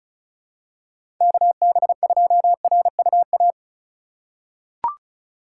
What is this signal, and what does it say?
Morse code, heard as a keyed tone, about 36 wpm, 700 Hz. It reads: KB2RUA
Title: USA Call Signs - 6 Characters - 4x Character Spacing - 35wpm